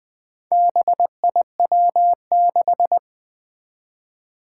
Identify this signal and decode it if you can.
Morse code, heard as a keyed tone, about 20 wpm, 700 Hz. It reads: BIW6
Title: Four Challenging Letter-Number Combinations 20wpm